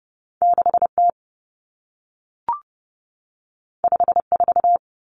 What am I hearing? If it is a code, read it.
Morse code, heard as a keyed tone, about 30 wpm, 700 Hz. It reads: 6T 54
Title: Common Cut Numbers - 2-Digit 30wpm